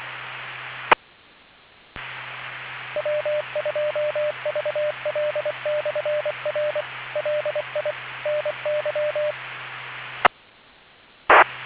Morse code, heard as a keyed tone, about 24 wpm, 600 Hz, with noise. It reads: E W2VL/R LI NY E T